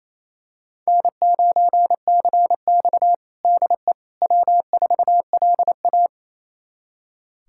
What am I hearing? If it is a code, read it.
Morse code, heard as a keyed tone, about 28 wpm, 700 Hz. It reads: N9CX DE W4LA
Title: USA Callsign DE Callsign - 4 Characters - 28wpm